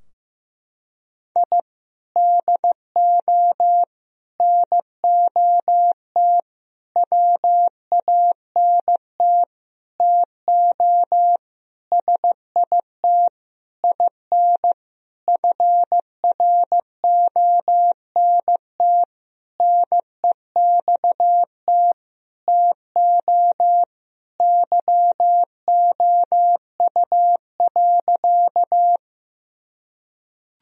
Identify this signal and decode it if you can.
Morse code, heard as a keyed tone, about 15 wpm, 700 Hz. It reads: I DO NOT WANT TO SIT IN FRONT NEXT TO YOU.
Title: Sentences from Top 1000 Words - Review Part 2 - 15wpm